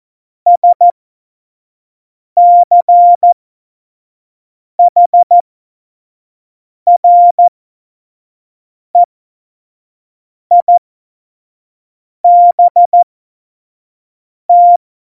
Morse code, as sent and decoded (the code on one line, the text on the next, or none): ... -.-. .... .-. .. -... -
SCHREIBT